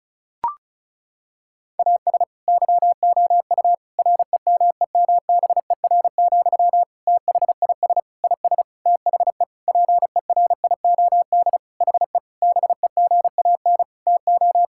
Morse code, heard as a keyed tone, about 36 wpm, 700 Hz. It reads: AS YOU REMEMBER, THIS IS THE PERIOD HE BEGAN TO
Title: Sentences from Top 1000 Words - Review Part 5 - 35wpm